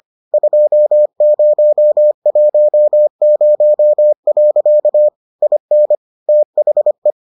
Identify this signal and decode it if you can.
Morse code, heard as a keyed tone, about 25 wpm, 600 Hz. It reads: 2010. IN THE